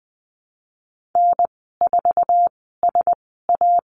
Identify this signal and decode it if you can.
Morse code, heard as a keyed tone, about 20 wpm, spaced as for 14 wpm, 700 Hz. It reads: N4SA
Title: 062 Lesson 11g - Letter h - Call Signs 20wpm